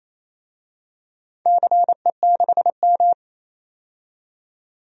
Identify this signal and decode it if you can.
Morse code, heard as a keyed tone, about 28 wpm, 700 Hz. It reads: CE6M
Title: DXCC-100 Call Signs - 4 Characters - 28wpm